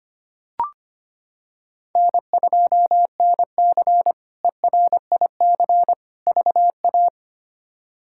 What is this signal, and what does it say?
Morse code, heard as a keyed tone, about 25 wpm, 700 Hz. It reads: N2NC ERIC VA
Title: K1USN SST QSO Practice - 25wpm